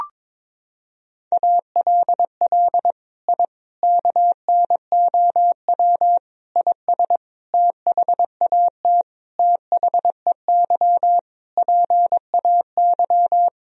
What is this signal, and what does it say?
Morse code, heard as a keyed tone, about 22 wpm, 700 Hz. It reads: ALL I KNOW IS THAT THEY PAY
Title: Sentences from Top 1000 Words - Review Part 1 - 22wpm